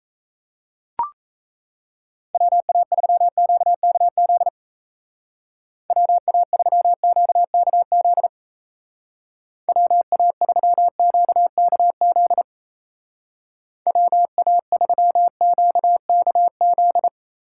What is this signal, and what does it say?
Morse code, heard as a keyed tone, about 34 wpm, 700 Hz. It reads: WA3QKZ WA3QKZ WA3QKZ WA3QKZ
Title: Speed Racing - USA 2x3 Call Signs 28wpm